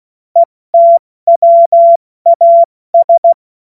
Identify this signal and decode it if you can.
Morse code, heard as a keyed tone, about 16 wpm, 680 Hz. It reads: ETWAS
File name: Morsekurs / Witze / Witz 17. Mai 2022 70 F80